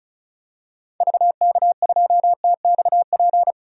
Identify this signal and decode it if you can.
Morse code, heard as a keyed tone, about 35 wpm, 700 Hz. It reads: VK2TXP
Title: DXCC-100 Call Signs - 6 Characters - 35wpm